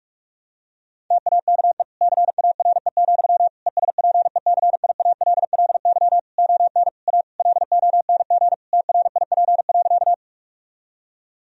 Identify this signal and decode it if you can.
Morse code, heard as a keyed tone, about 46 wpm, 700 Hz. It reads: TAKE CARE, ESPECIALLY ON A LONG TRIP.
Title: Sentences from Top 1000 Words - Review Part 4 - 45wpm